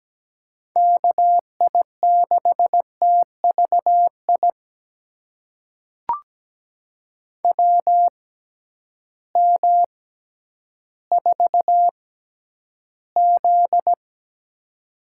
Morse code, heard as a keyed tone, about 17 wpm, spaced as for 10 wpm, 700 Hz. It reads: KI6TVI WM4Z
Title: USA Call Signs - 6x Character Spacing - 17wpm